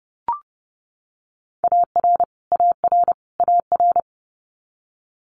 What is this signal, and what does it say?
Morse code, heard as a keyed tone, about 30 wpm, 700 Hz. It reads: AR AR AR
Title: Multi-letter Phonemes Repeated 3x 30wpm